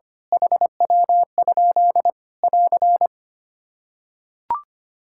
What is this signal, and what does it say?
Morse code, heard as a keyed tone, about 25 wpm, 700 Hz. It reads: HW? <AR>
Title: K1USN SST QSO Practice 25wpm